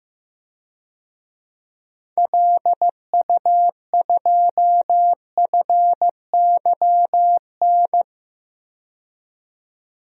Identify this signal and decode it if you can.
Morse code, heard as a keyed tone, about 15 wpm, 700 Hz. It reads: LU2FYN
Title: DXCC-100 Call Signs - 6 Characters - 15wpm